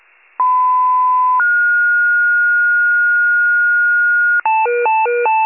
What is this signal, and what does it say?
Two-tone page, heard as a one-second tone, then a three-second tone, about 1010 then 1430 Hz. It is heard from 0.4 to 4.4 s.
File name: pager / 94 FIRE / wcalert